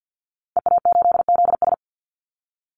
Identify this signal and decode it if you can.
Morse code, heard as a keyed tone, about 50 wpm, 700 Hz. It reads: EA8ZS
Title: Call Signs Repeated 2x - 50wpm